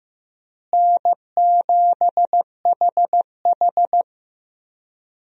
Morse code, heard as a keyed tone, about 15 wpm, 700 Hz. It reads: N7HH
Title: USA Call Signs - 4 Characters - Repeated 2x - 15wpm